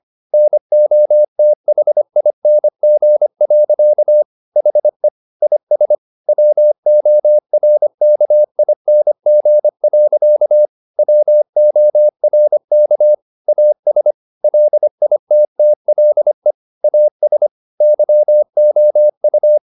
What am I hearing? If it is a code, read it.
Morse code, heard as a keyed tone, about 25 wpm, 600 Hz. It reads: NOTHING. HE IS WORKING. WORK AS LITTLE AS YOU
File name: the old man and the sea part 1 25wpm